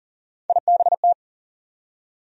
Morse code, heard as a keyed tone, about 40 wpm, 700 Hz. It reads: IBT
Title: USA Call Sign Postfixes - 3 Letters 40wpm